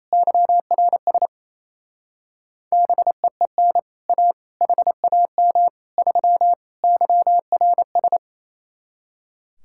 Morse code, heard as a keyed tone, about 36 wpm, 700 Hz. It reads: YRS BEEN A HAM 3 YRS